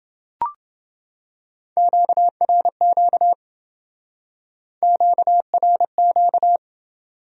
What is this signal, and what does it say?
Morse code, heard as a keyed tone, about 28 wpm, 700 Hz. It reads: QRQ QRQ